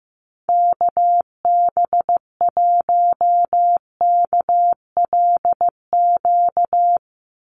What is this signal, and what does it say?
Morse code, heard as a keyed tone, about 15 wpm, 700 Hz. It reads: KB1KLQ
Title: USA Call Signs 15wpm